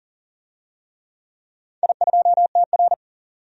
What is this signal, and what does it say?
Morse code, heard as a keyed tone, about 40 wpm, 700 Hz. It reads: I2TR
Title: DXCC-100 Call Signs - 4 Characters - 40wpm